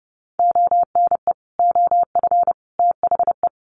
Morse code, heard as a keyed tone, about 30 wpm, 700 Hz. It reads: ONE OF THE
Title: Sets of 5 Words Encore2 30wpm